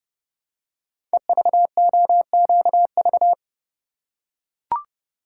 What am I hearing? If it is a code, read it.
Morse code, heard as a keyed tone, about 30 wpm, 700 Hz. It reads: EVOQV
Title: Code Groups - 5 Letters - 30wpm